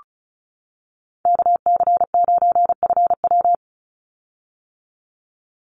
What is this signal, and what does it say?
Morse code, heard as a keyed tone, about 34 wpm, 700 Hz. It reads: KC9FW
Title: USA Call Signs 35wpm